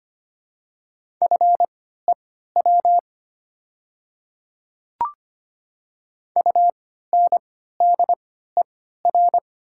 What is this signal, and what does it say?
Morse code, heard as a keyed tone, about 25 wpm, spaced as for 14 wpm, 690 Hz. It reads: FEW UNDER
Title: 3x Character Spacing - Top 300 Words - 25wpm